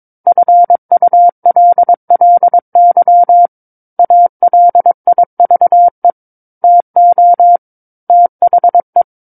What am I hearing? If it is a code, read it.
Morse code, heard as a keyed tone, about 22 wpm, 700 Hz. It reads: FULLY ALIVE TO THE
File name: Book One / 13-Chapter 13 - 22 WPM